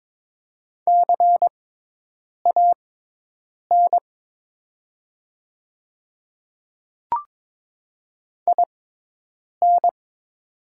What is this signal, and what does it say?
Morse code, heard as a keyed tone, about 22 wpm, spaced as for 8 wpm, 700 Hz. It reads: CAN IN
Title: 6x Character Spacing - Top 500 Words - 22wpm